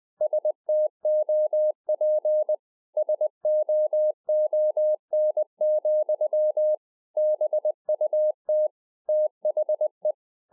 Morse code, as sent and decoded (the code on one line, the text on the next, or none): ... - --- .--. / ... --- --- -. --..-- / -... ..- - / - .... .
STOP SOON, BUT THE